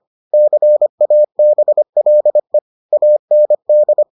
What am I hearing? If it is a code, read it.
Morse code, heard as a keyed tone, about 25 wpm, 600 Hz. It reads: CABLE AND